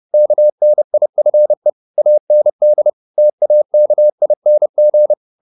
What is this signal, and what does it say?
Morse code, heard as a keyed tone, about 30 wpm, 600 Hz. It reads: KNIFE AND TAKING
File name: the old man and the sea / the old man and the sea 1 30wpm